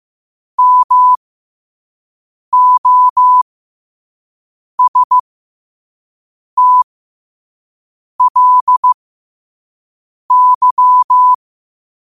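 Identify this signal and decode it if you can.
Morse code, heard as a keyed tone, about 15 wpm, spaced as for 5 wpm, 1000 Hz. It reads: MOSTLY
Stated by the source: Morse code sent at 5 words per minute.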